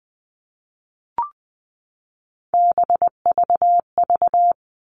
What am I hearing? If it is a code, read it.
Morse code, heard as a keyed tone, about 20 wpm, 700 Hz. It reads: BVV
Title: All Cut Numbers - 3-Digit 20wpm